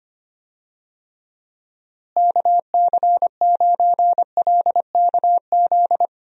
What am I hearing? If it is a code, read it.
Morse code, heard as a keyed tone, about 25 wpm, 700 Hz. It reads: KC9LKZ
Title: USA Call Signs - 6 Characters - Repeated 2x - 25wpm